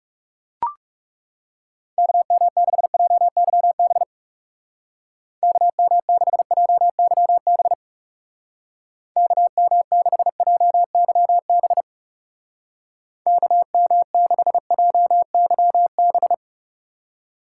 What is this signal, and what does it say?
Morse code, heard as a keyed tone, about 38 wpm, 700 Hz. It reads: KM6JYB KM6JYB KM6JYB KM6JYB